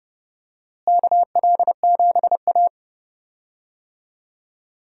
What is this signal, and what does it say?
Morse code, heard as a keyed tone, about 29 wpm, 700 Hz. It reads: KL7A
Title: DXCC-100 Call Signs - 4 Characters - Repeated 3x - 30wpm